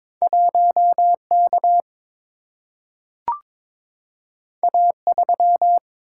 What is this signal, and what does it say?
Morse code, heard as a keyed tone, about 23 wpm, 700 Hz. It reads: 1 K A 3
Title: Single-Letter Single-Number 22wpm